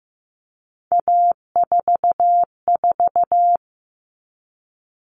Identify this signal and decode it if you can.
Morse code, heard as a keyed tone, about 15 wpm, 700 Hz. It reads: A44
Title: All Cut Numbers - 3-Digit 15wpm